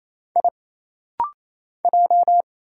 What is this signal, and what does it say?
Morse code, heard as a keyed tone, about 27 wpm, 700 Hz. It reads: I J
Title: Single Letters - Rapid-Fire 28wpm